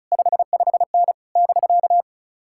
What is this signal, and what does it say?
Morse code, heard as a keyed tone, about 40 wpm, 700 Hz. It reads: 55N <BK>